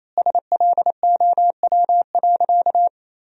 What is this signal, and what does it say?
Morse code, heard as a keyed tone, about 28 wpm, 700 Hz. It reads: SLOW.
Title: 4x Word-Spacing - Sentences from Top 400 Words - 28wpm